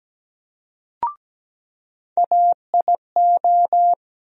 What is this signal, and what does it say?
Morse code, heard as a keyed tone, about 17 wpm, 700 Hz. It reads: AIO
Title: USA Call Sign Postfixes - 3 Letters 17wpm